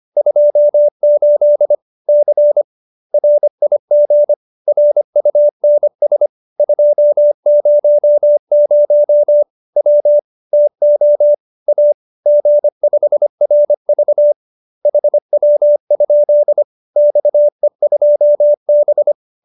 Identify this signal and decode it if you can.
Morse code, heard as a keyed tone, about 25 wpm, 590 Hz. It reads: 28 C RIG RUNS 200 W TO A G5RV HW? XE2B
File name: QSO304 25